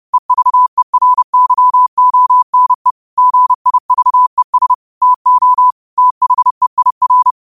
Morse code, heard as a keyed tone, about 30 wpm, 1000 Hz. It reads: EVERYONE GIVES TO THEIR